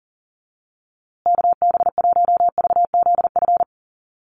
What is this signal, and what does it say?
Morse code, heard as a keyed tone, about 40 wpm, 700 Hz. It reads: KB1VZF
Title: USA Call Signs Repeated 2x 40wpm